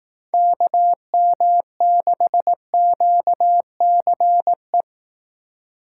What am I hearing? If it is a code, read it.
Morse code, heard as a keyed tone, about 18 wpm, 700 Hz. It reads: KM6QCE
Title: Speed Racing - USA 2x3 Call Signs 15wpm